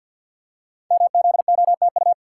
Morse code, heard as a keyed tone, about 52 wpm, 700 Hz. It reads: MZOTU
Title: Code Groups - 5 Letters - 50wpm